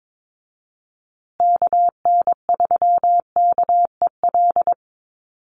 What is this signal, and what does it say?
Morse code, heard as a keyed tone, about 22 wpm, 700 Hz. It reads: KN3KEL
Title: USA Call Signs Repeated 3x 22wpm